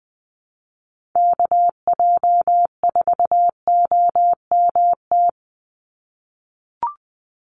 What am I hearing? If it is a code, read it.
Morse code, heard as a keyed tone, about 19 wpm, 700 Hz. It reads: KJ4OMT
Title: USA Call Signs 20wpm